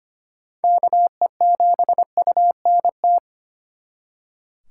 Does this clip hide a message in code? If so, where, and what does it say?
Yes: Morse code from 0.6 to 3.2 s, heard as a keyed tone, about 32 wpm, 700 Hz. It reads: KE7UNT